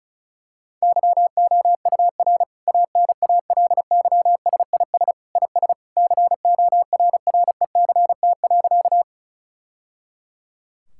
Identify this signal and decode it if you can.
Morse code, heard as a keyed tone, about 34 wpm, 700 Hz. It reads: YOUR ANALYSIS IS CORRECT.
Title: Sentences from Top 1000 Words - Review Part 5 - 35wpm